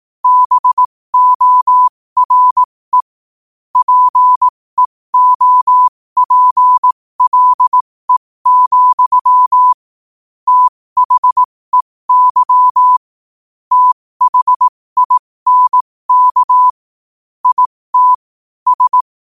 Morse code, heard as a keyed tone, about 18 wpm, spaced as for 15 wpm, 1000 Hz. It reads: BORE PEOPLE, THEY THINK IT S
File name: qotdfiles / 15wpm